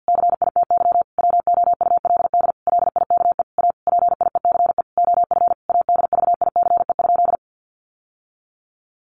Morse code, heard as a keyed tone, about 48 wpm, 700 Hz. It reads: CITY WOULD LIKE A PIECE OF ADVICE?
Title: Sentences from Top 1100 Words 50wpm